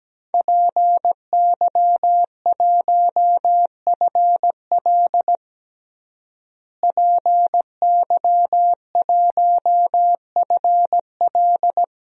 Morse code, heard as a keyed tone, about 17 wpm, 700 Hz. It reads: PY1FL PY1FL
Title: DXCC-100 Call Signs - 5 Characters - Repeated 2x - 17wpm